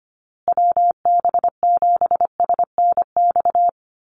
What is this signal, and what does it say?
Morse code, heard as a keyed tone, about 25 wpm, 700 Hz. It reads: WB7SNX
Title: USA Call Signs Repeated 3x 25wpm